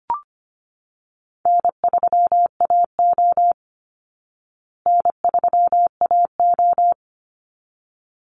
Morse code, heard as a keyed tone, about 25 wpm, 700 Hz. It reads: N3AO N3AO
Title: Easy Call Signs Repeated 2x - 25wpm